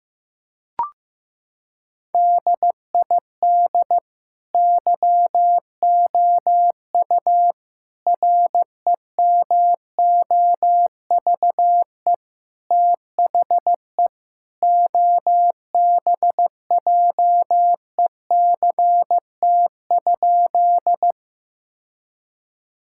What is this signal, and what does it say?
Morse code, heard as a keyed tone, about 15 wpm, 700 Hz. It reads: DID YOU REMOVE THE OBJECT?
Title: Sentences from Top 1000 Words - Review Part 5 - 15wpm